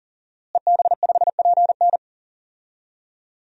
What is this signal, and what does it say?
Morse code, heard as a keyed tone, about 40 wpm, 700 Hz. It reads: EB5PN
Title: DXCC-100 Call Signs - 5 Characters - Repeated 2x - 40wpm